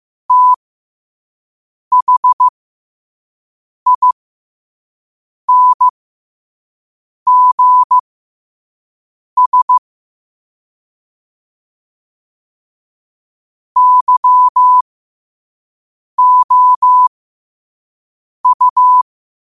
Morse code, heard as a keyed tone, about 15 wpm, spaced as for 5 wpm, 1000 Hz. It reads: THINGS YOU
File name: qotdfiles / 5wpm